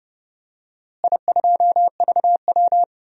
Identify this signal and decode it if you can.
Morse code, heard as a keyed tone, about 30 wpm, 700 Hz. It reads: I2VW
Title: DXCC-100 Call Signs - 4 Characters - Repeated 3x - 30wpm